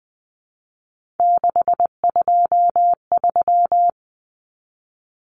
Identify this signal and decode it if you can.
Morse code, heard as a keyed tone, about 20 wpm, 700 Hz. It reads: 623
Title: Common Cut Numbers - 3-Digit 20wpm